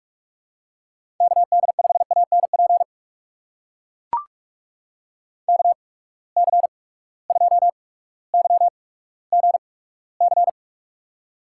Morse code, heard as a keyed tone, about 44 wpm, spaced as for 19 wpm, 700 Hz. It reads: KD5ANP KC2YGC
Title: USA Call Signs - 6 Characters - 8x Character Spacing - 45wpm